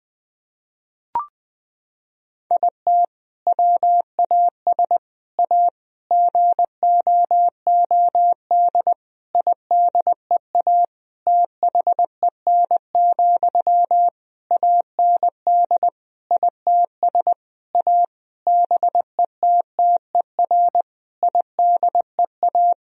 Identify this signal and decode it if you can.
Morse code, heard as a keyed tone, about 20 wpm, 700 Hz. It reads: IT WAS A GOOD IDEA THEN, AND ITS A BETTER IDEA
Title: Sentences from Top 1000 Words - Review Part 4 - 20wpm